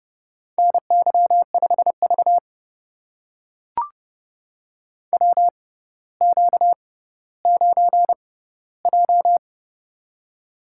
Morse code, heard as a keyed tone, about 30 wpm, spaced as for 15 wpm, 700 Hz. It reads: NY5V WQ9J
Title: USA Call Signs - 4 Characters - 6x Character Spacing - 30wpm